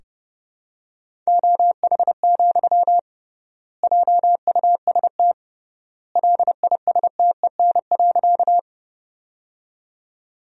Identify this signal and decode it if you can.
Morse code, heard as a keyed tone, about 30 wpm, 700 Hz. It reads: OH, JUST LISTEN.
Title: 3x Word-Spacing - Sentences from Top 500 Words - 30wpm